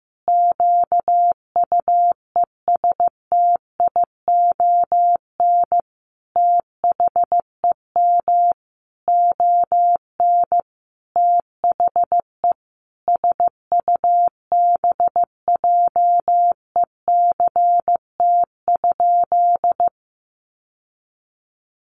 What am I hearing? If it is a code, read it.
Morse code, heard as a keyed tone, about 15 wpm, 700 Hz. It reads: QUESTION THEM ON THE SUBJECT?
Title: Sentences from Top 800 Words 15wpm